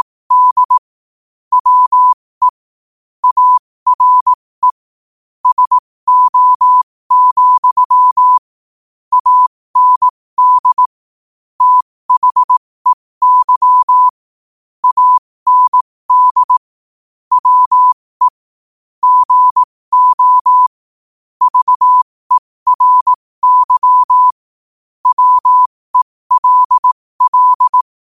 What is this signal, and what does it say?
Morse code, heard as a keyed tone, about 18 wpm, spaced as for 15 wpm, 1000 Hz. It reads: D WE ARE SO, AND THEY AND WE GO VERY WELL
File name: 15wpm